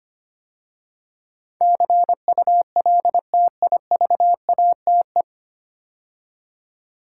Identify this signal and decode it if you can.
Morse code, heard as a keyed tone, about 25 wpm, 700 Hz. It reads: CULTIVATE
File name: practice / 25wpm / 9 Letter Words - First 3 Letters Given - 25wpm